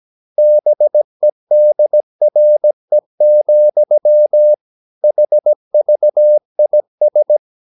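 Morse code, heard as a keyed tone, about 17 wpm, 600 Hz. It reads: BEDRE, HVIS
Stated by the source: Slow (QRS) Morse practice at 17 wpm, from CW med Gnister.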